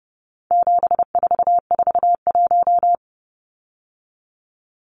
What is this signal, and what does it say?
Morse code, heard as a keyed tone, about 30 wpm, 700 Hz. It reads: 7441
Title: Numbers - 4-Digit 30wpm